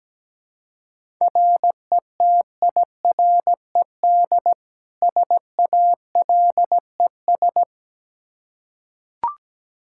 Morse code, heard as a keyed tone, about 17 wpm, 700 Hz. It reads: RETIRED SALES
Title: Practice QSO - Exchange 3 - More Info - 17wpm